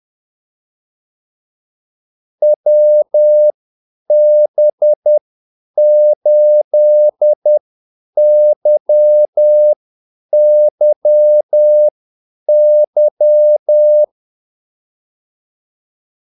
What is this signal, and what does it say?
Morse code, heard as a keyed tone, about 10 wpm, spaced as for 8 wpm, 600 Hz. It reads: WB8YYY
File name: Callsigns-10-8wpm